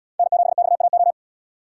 Morse code, heard as a keyed tone, about 69 wpm, 700 Hz. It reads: N7XAO